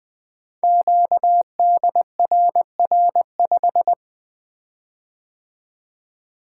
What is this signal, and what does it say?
Morse code, heard as a keyed tone, about 20 wpm, 700 Hz. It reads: QDRR5
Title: Code Groups - 5 Letter:Numbers - 20wpm